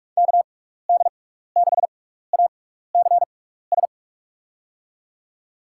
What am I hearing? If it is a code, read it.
Morse code, heard as a keyed tone, about 46 wpm, spaced as for 16 wpm, 700 Hz. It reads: KD6ACS
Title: USA Call Signs - 6x Character Spacing - 45wpm